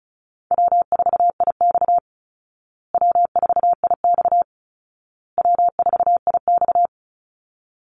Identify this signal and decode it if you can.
Morse code, heard as a keyed tone, about 34 wpm, 700 Hz. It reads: W4IX W4IX W4IX
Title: Call Signs Repeated 2x - 35wpm